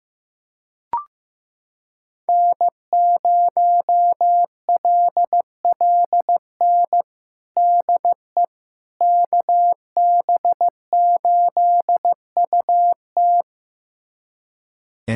Morse code, heard as a keyed tone, about 15 wpm, 700 Hz. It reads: N0LLN DE KB8UT
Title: USA Callsign DE Callsign - 5 Characters - 15wpm